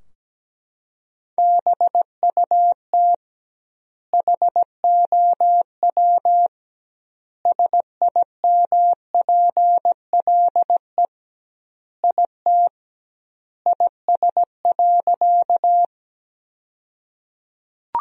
Morse code, heard as a keyed tone, about 17 wpm, 700 Hz. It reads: BUT HOW SIMPLE IT IS.
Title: 2x Word-Spacing - Sentences from Top 400 Words - 17wpm